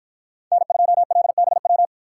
Morse code, heard as a keyed tone, about 53 wpm, 700 Hz. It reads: N2LBW